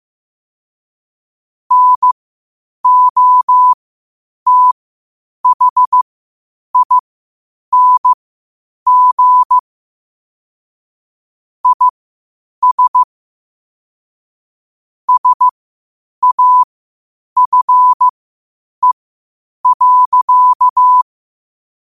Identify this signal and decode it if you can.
Morse code, heard as a keyed tone, about 15 wpm, spaced as for 8 wpm, 1000 Hz. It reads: NOTHING IS SAFE.
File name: qotdfiles / 8wpm